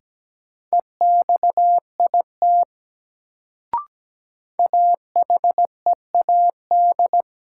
Morse code, heard as a keyed tone, about 17 wpm, 700 Hz. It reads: EXIT AHEAD
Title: Sentences from Top 1100 Words - New Word Drill Alt - 17wpm